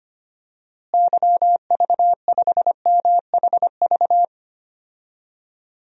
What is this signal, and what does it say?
Morse code, heard as a keyed tone, about 25 wpm, 700 Hz. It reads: YV5MHV
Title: DXCC-100 Call Signs - 6 Characters - Repeated 3x - 25wpm